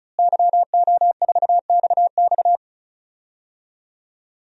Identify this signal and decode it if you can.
Morse code, heard as a keyed tone, about 36 wpm, 700 Hz. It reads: YO4XX
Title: DXCC-100 Call Signs - 5 Characters - Repeated 3x - 35wpm